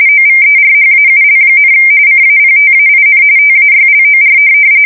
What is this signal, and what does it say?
RTTY, about 45.45 baud. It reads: OVERHAUL, CAPTAIN TAYLOR
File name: fsk170 / USS-Missouri